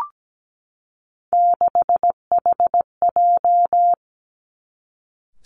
Morse code, heard as a keyed tone, about 17 wpm, 700 Hz. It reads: 6HJ
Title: Three Challenging Letter-Number Combinations 17wpm